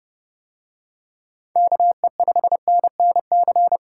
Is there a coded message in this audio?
Yes: Morse code from 1.6 to 3.8 s, heard as a keyed tone, about 30 wpm, 700 Hz. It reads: KE5NNC